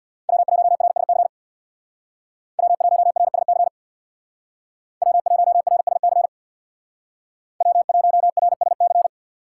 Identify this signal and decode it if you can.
Morse code, heard as a keyed tone, about 64 wpm, 700 Hz. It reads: W1RSC W1RSC W1RSC W1RSC